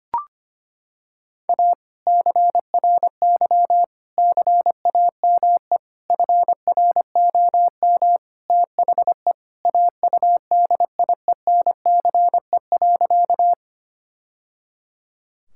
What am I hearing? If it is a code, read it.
Morse code, heard as a keyed tone, about 25 wpm, 700 Hz. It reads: A CRY CAME FROM THE AUDIENCE.
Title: Sentences from Top 1000 Words - Review Part 5 - 25wpm